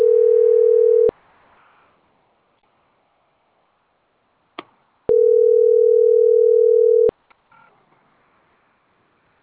Ringback tone (North America), 440 + 480 Hz.